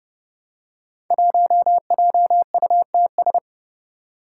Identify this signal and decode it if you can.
Morse code, heard as a keyed tone, about 29 wpm, 700 Hz. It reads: 1JUTS
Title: Five Challenging Letter-Number Combinations 30wpm